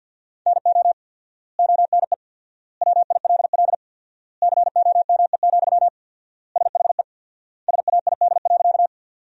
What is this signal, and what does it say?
Morse code, heard as a keyed tone, about 50 wpm, 700 Hz. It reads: NO ONE WILL COME, SHE SAID.
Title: 4x Word-Spacing - Sentences from Top 100 Words - 50wpm